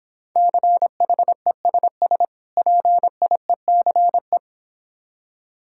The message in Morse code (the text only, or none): CHESS PIECE